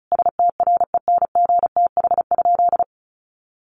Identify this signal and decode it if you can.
Morse code, heard as a keyed tone, about 34 wpm, 700 Hz. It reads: STRENGTH?
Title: Sentences from Top 1300 Words 35wpm